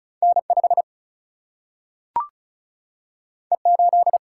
Morse code, heard as a keyed tone, about 36 wpm, 700 Hz. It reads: N5 E8
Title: Single-Letter Single-Number 35wpm